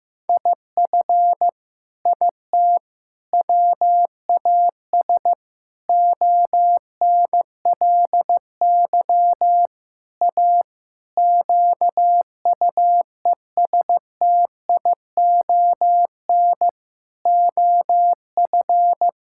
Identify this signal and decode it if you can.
Morse code, heard as a keyed tone, about 15 wpm, 700 Hz. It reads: IF IT WAS ONLY A QUESTION OF
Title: Sentences from Top 1600 Words 15wpm